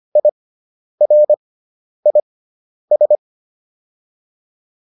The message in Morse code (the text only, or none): IRIS